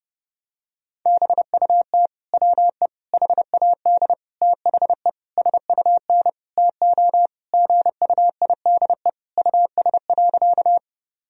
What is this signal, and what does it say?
Morse code, heard as a keyed tone, about 31 wpm, 700 Hz. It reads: BUT WE HAD THE SUN TO GUIDE US.
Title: Sentences from Top 1000 Words - Review Part 3 - 30wpm